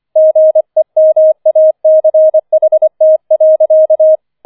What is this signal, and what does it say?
Morse code, heard as a keyed tone, about 25 wpm, 630 Hz. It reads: GEMACHT.
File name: 25wpm